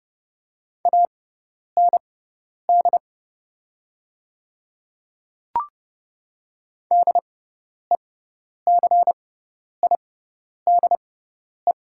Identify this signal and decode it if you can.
Morse code, heard as a keyed tone, about 30 wpm, spaced as for 10 wpm, 700 Hz. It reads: AND DECIDE
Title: 6x Character Spacing - Top 500 Words - 30wpm